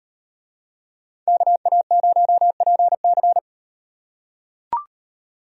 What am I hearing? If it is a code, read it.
Morse code, heard as a keyed tone, about 38 wpm, 700 Hz. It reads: KA0PC